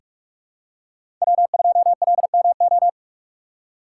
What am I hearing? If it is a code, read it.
Morse code, heard as a keyed tone, about 46 wpm, 700 Hz. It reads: W2LMO
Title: USA Call Signs - 5 Characters - 2x Character Spacing - 45wpm